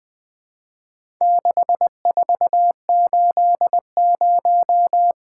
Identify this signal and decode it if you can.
Morse code, heard as a keyed tone, about 21 wpm, 700 Hz. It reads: 6480
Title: Four Letter-Number Combinations - 20wpm